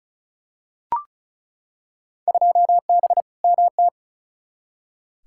Morse code, heard as a keyed tone, about 36 wpm, 700 Hz. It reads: 2B MT
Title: Field Day Exchanges Encore - 35wpm